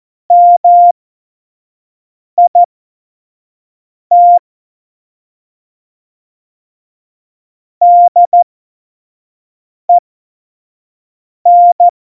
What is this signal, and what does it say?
Morse code, heard as a keyed tone, about 14 wpm, spaced as for 5 wpm, 700 Hz. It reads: MIT DEN